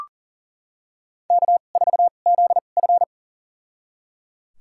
Morse code, heard as a keyed tone, about 37 wpm, spaced as for 28 wpm, 700 Hz. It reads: K4ZF
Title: USA Call Signs - 4 Characters - 2x Character Spacing - 40wpm